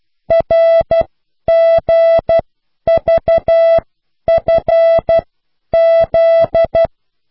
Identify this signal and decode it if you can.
Morse code, heard as a keyed tone, about 12 wpm, spaced as for 10 wpm, 660 Hz. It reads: RGVFZ